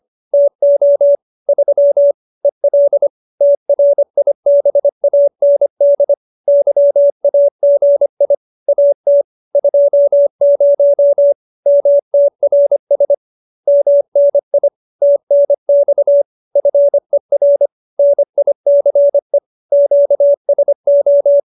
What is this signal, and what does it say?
Morse code, heard as a keyed tone, about 25 wpm, 570 Hz. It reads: TO 3 EL TRIBAND YAGI AT 20 MTRS MNI TNX FER NICE QSO